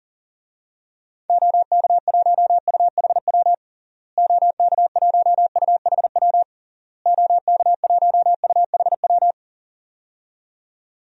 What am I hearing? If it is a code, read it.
Morse code, heard as a keyed tone, about 40 wpm, 700 Hz. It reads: OK1UHW OK1UHW OK1UHW
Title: DXCC-100 Call Signs - 6 Characters - Repeated 3x - 40wpm